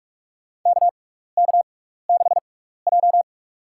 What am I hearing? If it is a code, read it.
Morse code, heard as a keyed tone, about 46 wpm, spaced as for 16 wpm, 700 Hz. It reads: KK6J
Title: USA Call Signs - 4 Characters - 6x Character Spacing - 45wpm